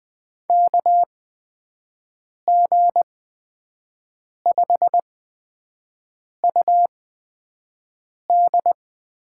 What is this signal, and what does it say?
Morse code, heard as a keyed tone, about 20 wpm, spaced as for 5 wpm, 700 Hz. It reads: KG5UD
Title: USA Call Signs - 6 Characters - 8x Character Spacing - 20wpm